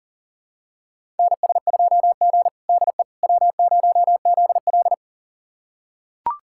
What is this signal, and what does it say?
Morse code, heard as a keyed tone, about 40 wpm, 700 Hz. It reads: NS2G DE W0ZL
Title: USA Callsign DE Callsign - 4 Characters - 40wpm